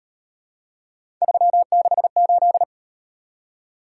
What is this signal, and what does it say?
Morse code, heard as a keyed tone, about 37 wpm, 700 Hz. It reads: 368